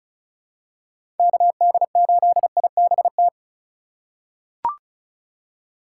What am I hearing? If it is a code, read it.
Morse code, heard as a keyed tone, about 36 wpm, 700 Hz. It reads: KD8IBT
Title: USA Call Signs - 4x Character Spacing - 35wpm